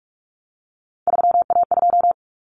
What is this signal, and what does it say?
Morse code, heard as a keyed tone, about 44 wpm, 700 Hz. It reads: 3A2
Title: Common Cut Numbers - 3-Digit 45wpm